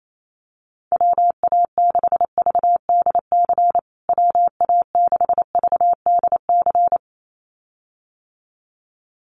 Morse code, heard as a keyed tone, about 28 wpm, 700 Hz. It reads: WA6VDC WA6VDC
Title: USA Call Signs Repeated 2x 28wpm